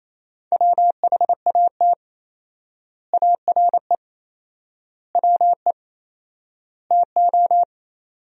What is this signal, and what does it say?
Morse code, heard as a keyed tone, about 27 wpm, 700 Hz. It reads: WHAT ARE WE TO
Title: 4x Word-Spacing - Sentences from Top 100 Words - 28wpm